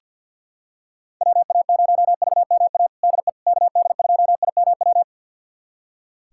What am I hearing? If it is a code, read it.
Morse code, heard as a keyed tone, about 47 wpm, 690 Hz. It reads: WA0VMA DE KD2IMW
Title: USA Callsign DE Callsign 50wpm